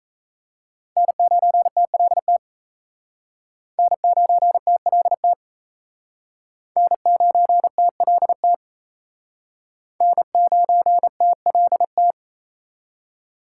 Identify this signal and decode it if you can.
Morse code, heard as a keyed tone, about 35 wpm, 700 Hz. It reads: N9TLT N9TLT N9TLT N9TLT